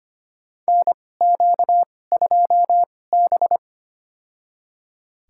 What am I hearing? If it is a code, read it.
Morse code, heard as a keyed tone, about 25 wpm, spaced as for 18 wpm, 700 Hz. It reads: NQ2B
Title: USA Call Signs - 4 Characters - 2x Character Spacing - 25wpm